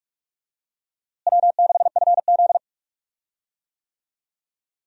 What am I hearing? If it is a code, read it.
Morse code, heard as a keyed tone, about 42 wpm, 700 Hz. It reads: W6FZ
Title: USA Call Signs - 4 Characters - Repeated 2x - 45wpm